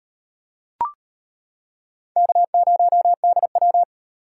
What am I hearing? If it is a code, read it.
Morse code, heard as a keyed tone, about 38 wpm, 700 Hz. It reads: K0DW